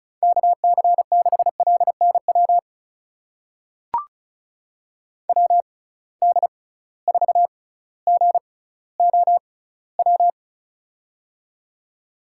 Morse code, heard as a keyed tone, about 34 wpm, spaced as for 18 wpm, 700 Hz. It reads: KC6LNW WD4GOW
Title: USA Call Signs - 6 Characters - 6x Character Spacing - 35wpm